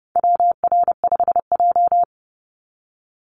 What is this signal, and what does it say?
Morse code, heard as a keyed tone, about 30 wpm, 700 Hz. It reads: WR5J
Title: Call Signs - 30wpm